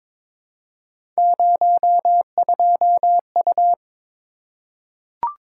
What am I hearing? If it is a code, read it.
Morse code, heard as a keyed tone, about 22 wpm, 700 Hz. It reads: 02U
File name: practice / 22wpm / Three Letter-Number Combinations - 22wpm